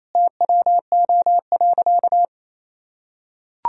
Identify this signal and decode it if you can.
Morse code, heard as a keyed tone, about 28 wpm, 700 Hz. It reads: TWO.
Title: Sentences from Top 1000 Words - Review Part 1 - 28wpm